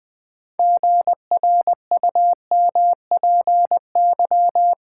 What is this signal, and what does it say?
Morse code, heard as a keyed tone, about 21 wpm, 700 Hz. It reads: GRUMPY
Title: Pangram Challenge v2 - 20wpm